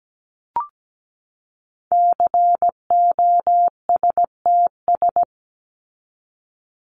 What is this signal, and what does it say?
Morse code, heard as a keyed tone, about 17 wpm, 700 Hz. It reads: COSTS
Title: ES:S Suffix - Top 100 Words with Base Words 17wpm